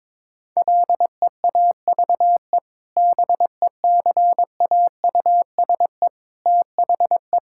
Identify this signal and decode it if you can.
Morse code, heard as a keyed tone, about 22 wpm, 700 Hz. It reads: LEAVE BECAUSE THE
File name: practice / 22wpm / Sentences with 2-Word Phrases - 22wpm